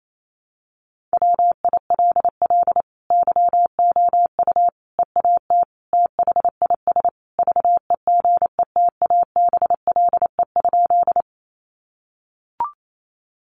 Morse code, heard as a keyed tone, about 28 wpm, 700 Hz. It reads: WILL YOU EAT THIS VEGETABLE?
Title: Sentences from Top 1300 Words 28wpm